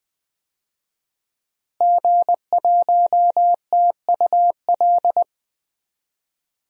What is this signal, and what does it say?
Morse code, heard as a keyed tone, about 20 wpm, 700 Hz. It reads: G1TUL
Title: DXCC-100 Call Signs - 5 Characters - Repeated 3x - 20wpm